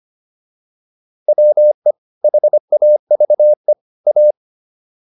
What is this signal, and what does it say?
Morse code, heard as a keyed tone, about 25 wpm, 600 Hz. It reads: WE HAVE A